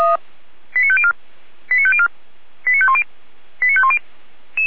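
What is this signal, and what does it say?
Five-tone selective call (ZVEI), heard as quick runs of five tones, tones of about 70 ms. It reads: 89504 89504 89510 89510
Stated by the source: Five-tone ZVEI selective call received on SDR.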